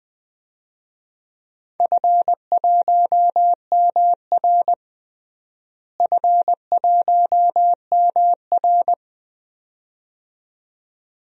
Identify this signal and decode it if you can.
Morse code, heard as a keyed tone, about 20 wpm, 700 Hz. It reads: F1MR F1MR
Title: DXCC-100 Call Signs - 4 Characters - Repeated 2x - 20wpm